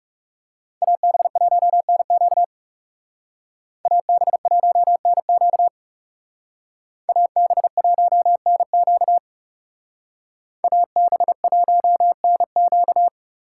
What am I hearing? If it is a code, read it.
Morse code, heard as a keyed tone, about 36 wpm, 700 Hz. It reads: AB1NQ AB1NQ AB1NQ AB1NQ
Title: Speed Racing - USA 2x2 Call Signs 30wpm